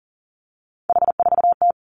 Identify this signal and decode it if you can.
Morse code, heard as a keyed tone, about 40 wpm, 700 Hz. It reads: H4T
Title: Three Challenging Letter-Number Combinations 40wpm